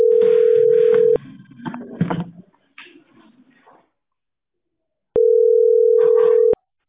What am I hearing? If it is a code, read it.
Ringback tone (North America), 440 + 480 Hz.